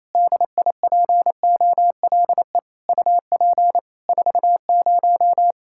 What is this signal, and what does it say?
Morse code, heard as a keyed tone, about 28 wpm, 700 Hz. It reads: DIPOLE UP 40
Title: K1USN SST QSO Practice 28wpm